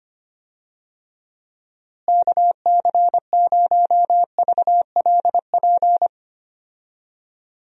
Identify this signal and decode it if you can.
Morse code, heard as a keyed tone, about 25 wpm, 700 Hz. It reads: KC0VLP
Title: USA Call Signs - 6 Characters - Repeated 2x - 25wpm